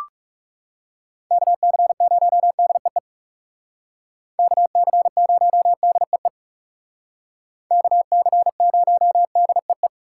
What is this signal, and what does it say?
Morse code, heard as a keyed tone, about 37 wpm, 700 Hz. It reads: KC0DEE KC0DEE KC0DEE
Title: Speed Racing - USA 2x3 Call Signs 30wpm